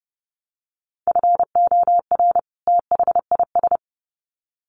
Morse code, heard as a keyed tone, about 30 wpm, 700 Hz. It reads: FOR THIS
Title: Sets of 2 Words 30wpm